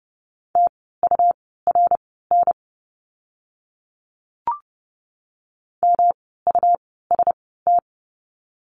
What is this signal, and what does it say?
Morse code, heard as a keyed tone, about 30 wpm, spaced as for 17 wpm, 700 Hz. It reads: TURN MUST
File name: course / 30:10wpm / 115 Lesson 19f - Letter w - Words 30wpm